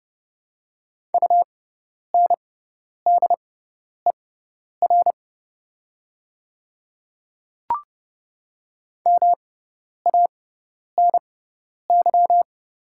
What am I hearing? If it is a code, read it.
Morse code, heard as a keyed tone, about 30 wpm, spaced as for 10 wpm, 700 Hz. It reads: UNDER MANY
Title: 6x Character Spacing - Top 200 Words - 30wpm